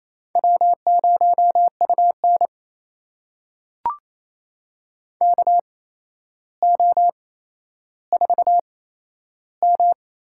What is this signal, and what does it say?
Morse code, heard as a keyed tone, about 28 wpm, spaced as for 12 wpm, 700 Hz. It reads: W0UN KO4M
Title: USA Call Signs - 4 Characters - 8x Character Spacing - 28wpm